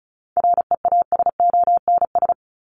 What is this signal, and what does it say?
Morse code, heard as a keyed tone, about 34 wpm, 700 Hz. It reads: REASONS
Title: ES:S Suffix - Top 100 Words with Base Words 35wpm